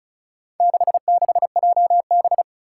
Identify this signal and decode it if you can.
Morse code, heard as a keyed tone, about 34 wpm, 700 Hz. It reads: 66JB
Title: Four Challenging Letter-Number Combinations 35wpm